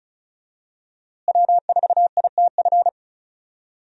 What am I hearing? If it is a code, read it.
Morse code, heard as a keyed tone, about 35 wpm, 700 Hz. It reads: W4ITF
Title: USA Call Signs - 5 Characters - Repeated 3x - 35wpm